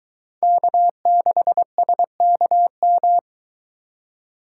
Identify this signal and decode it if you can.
Morse code, heard as a keyed tone, about 22 wpm, 700 Hz. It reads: K6SKM